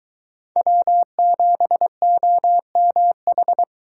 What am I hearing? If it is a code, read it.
Morse code, heard as a keyed tone, about 23 wpm, 700 Hz. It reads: W7OMH